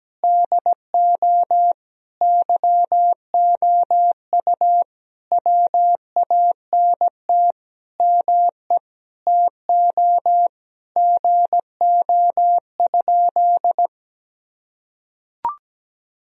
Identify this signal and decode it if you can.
Morse code, heard as a keyed tone, about 17 wpm, 700 Hz. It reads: DO YOU WANT ME TO GO?
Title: Sentences from Top 1000 Words - Review Part 3 - 17wpm